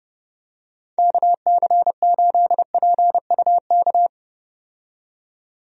Morse code, heard as a keyed tone, about 31 wpm, 700 Hz. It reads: KC8PUK
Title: USA Call Signs - 6 Characters - 30wpm